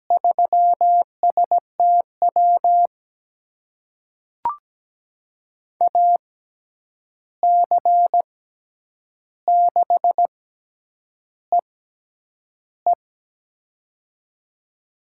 Morse code, heard as a keyed tone, about 17 wpm, spaced as for 8 wpm, 700 Hz. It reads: 3STW AC6EE
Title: USA Call Signs - 5 Characters - 6x Character Spacing - 17wpm